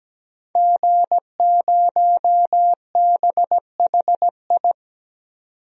Morse code, heard as a keyed tone, about 17 wpm, 700 Hz. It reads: G0BHI
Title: DXCC-100 Call Signs - 5 Characters - 17wpm